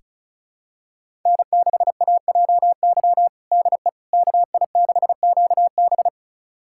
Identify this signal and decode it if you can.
Morse code, heard as a keyed tone, about 34 wpm, 700 Hz. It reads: N6AJY DE KI6QB
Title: USA Callsign DE Callsign - 5 Characters - 35wpm